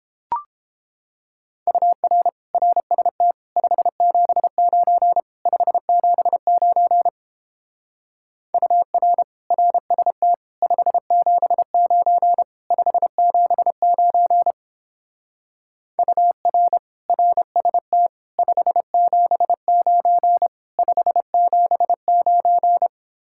Morse code, heard as a keyed tone, about 29 wpm, 700 Hz. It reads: UR RST 579 579 UR RST 579 579 UR RST 579 579